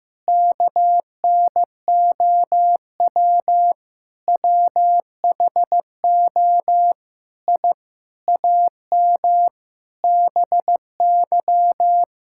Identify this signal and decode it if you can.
Morse code, heard as a keyed tone, about 15 wpm, 700 Hz. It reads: KNOW WHO I AM BY
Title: Sentences from Top 1000 Words - Review Part 5 - 15wpm